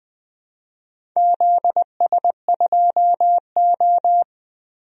Morse code, heard as a keyed tone, about 20 wpm, 700 Hz. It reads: ZS2O
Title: DXCC-100 Call Signs - 4 Characters - 20wpm